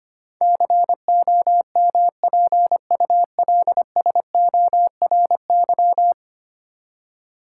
Morse code, heard as a keyed tone, about 25 wpm, 700 Hz. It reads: COMPULSORY
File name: 10 Letter Words - First 3 Letters Given - 25wpm